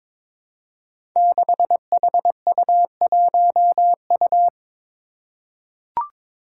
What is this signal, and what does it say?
Morse code, heard as a keyed tone, about 22 wpm, 700 Hz. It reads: 6HU1U
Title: Five Challenging Letter-Number Combinations 22wpm